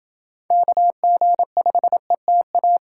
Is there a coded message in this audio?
Yes: Morse code from 0.5 to 2.8 s, heard as a keyed tone, about 29 wpm, 700 Hz. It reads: KG5ETA